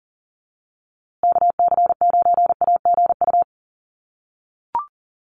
Morse code, heard as a keyed tone, about 40 wpm, 700 Hz. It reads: KC9AGU
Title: USA Call Signs Repeated 2x 40wpm